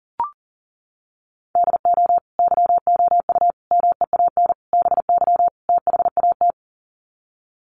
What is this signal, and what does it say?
Morse code, heard as a keyed tone, about 40 wpm, 700 Hz. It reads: DO YOU MEAN BY THAT
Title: Sets of 5 Words Encore2 40wpm